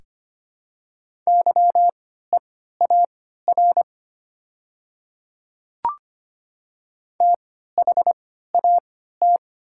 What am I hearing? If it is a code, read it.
Morse code, heard as a keyed tone, about 25 wpm, spaced as for 14 wpm, 700 Hz. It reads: YEAR THAT
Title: 3x Character Spacing - Top 200 Words - 25wpm